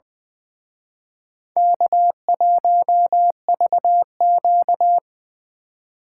Morse code, heard as a keyed tone, about 24 wpm, 700 Hz. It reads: K1VQ
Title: Speed Racing - USA 1x2 Call Signs 20wpm